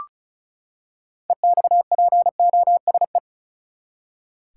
Morse code, heard as a keyed tone, about 36 wpm, 700 Hz. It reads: EXPOSE
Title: Six Letter Words - 35wpm